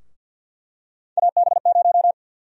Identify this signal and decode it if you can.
Morse code, heard as a keyed tone, about 46 wpm, 700 Hz. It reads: AB0
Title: USA Call Sign Prefixes - 50wpm